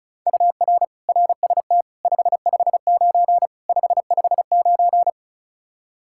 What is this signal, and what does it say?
Morse code, heard as a keyed tone, about 36 wpm, 700 Hz. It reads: UR RST 559 559
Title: Practice QSO - Exchange 1 - Beginning - 35wpm